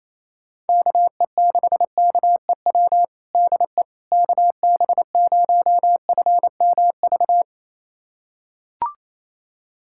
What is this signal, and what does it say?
Morse code, heard as a keyed tone, about 28 wpm, 700 Hz. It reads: KE6KEW DE KB0FMV
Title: USA Callsign DE Callsign 28wpm